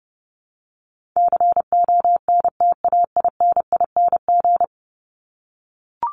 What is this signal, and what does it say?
Morse code, heard as a keyed tone, about 30 wpm, 700 Hz. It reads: CONTAINING
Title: ING Suffix - Top 100 Words with Base Words 30wpm